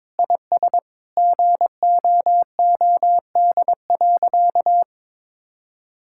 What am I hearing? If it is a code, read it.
Morse code, heard as a keyed tone, about 21 wpm, 700 Hz. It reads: IS GOOD.
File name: practice / 22wpm / Sentences with 4-Word Phrases - 22wpm